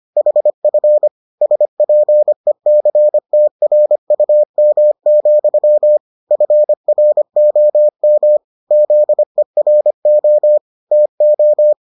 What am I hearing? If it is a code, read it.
Morse code, heard as a keyed tone, about 25 wpm, 600 Hz. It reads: HF SPECTRUM, FROM ZERO TO